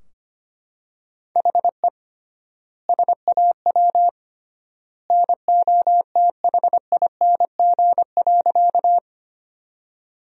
Morse code, heard as a keyed tone, about 25 wpm, 700 Hz. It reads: HE SAW NOTHING.
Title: 3x Word-Spacing - Sentences from Top 500 Words - 25wpm